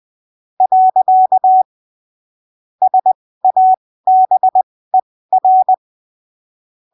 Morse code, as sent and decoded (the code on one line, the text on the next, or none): .-.-.- / ... .- -... .-.
. SABER